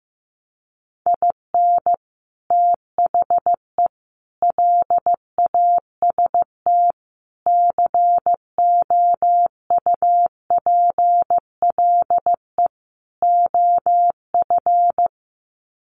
Morse code, heard as a keyed tone, about 15 wpm, 700 Hz. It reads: IN THE LAST COUPLE OF
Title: Sets of 5 Words 15wpm